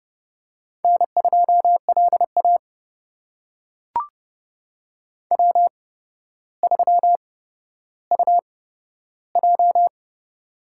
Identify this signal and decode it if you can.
Morse code, heard as a keyed tone, about 30 wpm, spaced as for 13 wpm, 700 Hz. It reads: N2LA W3UJ
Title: USA Call Signs - 4 Characters - 8x Character Spacing - 30wpm